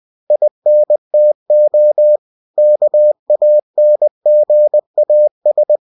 Morse code, heard as a keyed tone, about 20 wpm, 600 Hz. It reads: INTO KANGAS
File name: cw-books / winnie-the-pooh / winnie-20wpm-page-0069